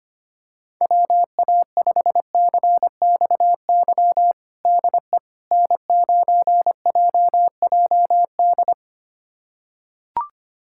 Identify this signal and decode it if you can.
Morse code, heard as a keyed tone, about 25 wpm, 700 Hz. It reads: WA5CXY DE N9JJD
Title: USA Callsign DE Callsign 25wpm